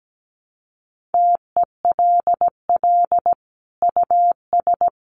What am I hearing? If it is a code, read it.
Morse code, heard as a keyed tone, about 17 wpm, 700 Hz. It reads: TELL US
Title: Sentences from Top 1400 Words 17wpm